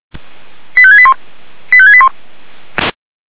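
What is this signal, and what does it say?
Five-tone selective call (ZVEI), heard as quick runs of five tones, tones of about 70 ms. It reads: 85681 85681
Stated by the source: ZVEI five-tone selective call received on SDR.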